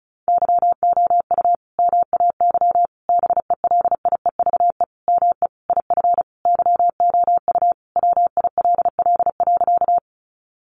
Morse code, heard as a keyed tone, about 34 wpm, 700 Hz. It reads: YOU MAY BELIEVE ME IF YOU WILL.
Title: Sentences from Top 1000 Words 35wpm